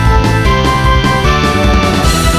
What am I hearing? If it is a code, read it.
There is no code.